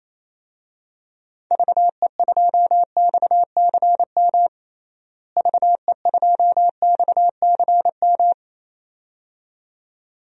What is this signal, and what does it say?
Morse code, heard as a keyed tone, about 28 wpm, 700 Hz. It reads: VE2XCM VE2XCM
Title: DXCC-100 Call Signs - 6 Characters - Repeated 2x - 28wpm